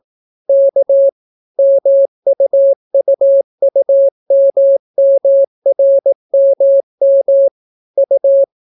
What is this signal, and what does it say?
Morse code, heard as a keyed tone, about 18 wpm, 550 Hz. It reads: K MUUUMMRMM U